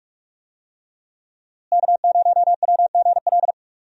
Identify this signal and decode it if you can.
Morse code, heard as a keyed tone, about 42 wpm, 700 Hz. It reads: K0WGL
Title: USA Call Signs - 5 Characters - Repeated 3x - 45wpm